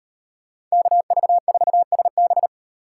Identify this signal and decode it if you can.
Morse code, heard as a keyed tone, about 38 wpm, 700 Hz. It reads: KV4SB